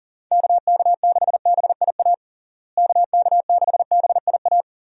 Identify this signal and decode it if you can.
Morse code, heard as a keyed tone, about 40 wpm, 700 Hz. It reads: KK6BIA KK6BIA
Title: USA Call Signs - 6 Characters - Repeated 3x - 40wpm